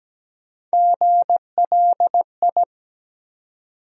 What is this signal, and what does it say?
Morse code, heard as a keyed tone, about 17 wpm, 700 Hz. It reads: GLI
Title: USA Call Sign Postfixes - 3 Letters 17wpm